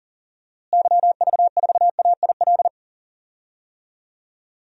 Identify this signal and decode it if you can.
Morse code, heard as a keyed tone, about 40 wpm, 700 Hz. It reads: YV4AIL
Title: DXCC-100 Call Signs - 6 Characters - Repeated 2x - 40wpm